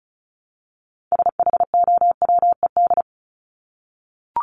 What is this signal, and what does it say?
Morse code, heard as a keyed tone, about 34 wpm, 700 Hz. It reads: SHOWED
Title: ED Suffix - Top 100 Words with Base Words 35wpm